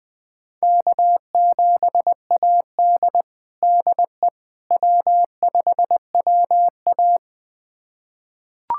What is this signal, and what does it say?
Morse code, heard as a keyed tone, about 20 wpm, 700 Hz. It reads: K7AD DE W5WA
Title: USA Callsign DE Callsign - 4 Characters - 20wpm